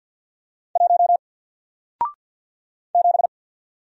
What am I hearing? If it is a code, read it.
Morse code, heard as a keyed tone, about 53 wpm, 700 Hz. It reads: 1 7
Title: Single Letter-Number 50wpm